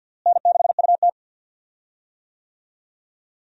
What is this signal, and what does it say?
Morse code, heard as a keyed tone, about 52 wpm, 700 Hz. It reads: N6UT
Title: USA Call Signs - 4 Characters - Repeated 2x - 50wpm